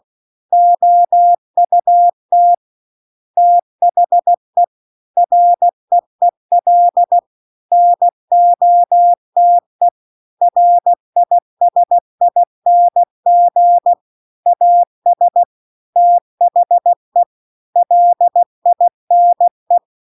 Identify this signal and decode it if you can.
Morse code, heard as a keyed tone, about 16 wpm, 700 Hz. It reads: OUT THE REEL NOTE RISING AS THE LINE
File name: CW-The-Big-Twoheaded-River-2-16WPM